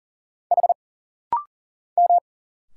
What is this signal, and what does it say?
Morse code, heard as a keyed tone, about 40 wpm, 700 Hz. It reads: H M
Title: Single Letters - Rapid-Fire 40wpm